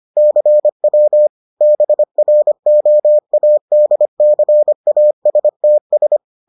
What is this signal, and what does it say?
Morse code, heard as a keyed tone, about 25 wpm, 600 Hz. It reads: CW BROADCASTS